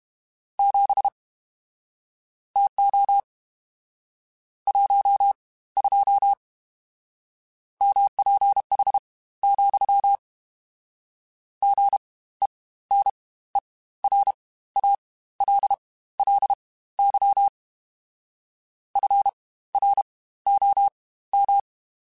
Morse code, heard as a keyed tone, about 32 wpm, spaced as for 17 wpm, 800 Hz. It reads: 7 TO 12 MPH, GENERALLY FROM
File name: Spaced Characters / Spaced Characters-mp3s / Weather5 22 sp @30